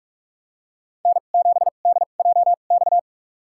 Morse code, heard as a keyed tone, about 46 wpm, spaced as for 32 wpm, 700 Hz. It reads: N7DJX
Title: USA Call Signs - 5 Characters - 2x Character Spacing - 45wpm